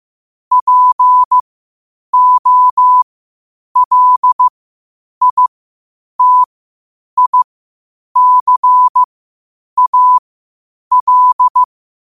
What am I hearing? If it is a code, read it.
Morse code, heard as a keyed tone, about 15 wpm, spaced as for 8 wpm, 1000 Hz. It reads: POLITICAL
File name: qotdfiles / 8wpm